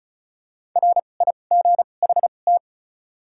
Morse code, heard as a keyed tone, about 34 wpm, spaced as for 25 wpm, 700 Hz. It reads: RIGHT
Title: 2x Character Spacing - Top 200 Words - 35wpm